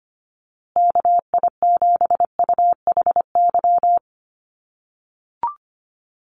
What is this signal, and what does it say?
Morse code, heard as a keyed tone, about 25 wpm, 700 Hz. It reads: KI7UHY
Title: USA Call Signs Repeated 2x 25wpm